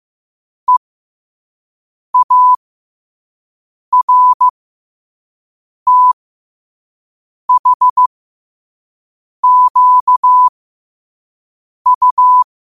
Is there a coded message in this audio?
Yes: Morse code from 0.7 to 12.4 s, heard as a keyed tone, about 15 wpm, spaced as for 5 wpm, 1000 Hz. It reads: EARTHQU